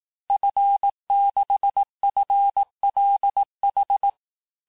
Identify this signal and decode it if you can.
Morse code, heard as a keyed tone, about 18 wpm, 800 Hz. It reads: F6FLH
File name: cours-du-lundi files / cours-1588-FIN-de-QSO-vitesse-17MPM